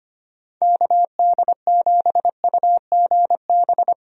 Morse code, heard as a keyed tone, about 25 wpm, 700 Hz. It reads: KD7UGB
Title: USA Call Signs - 6 Characters - 25wpm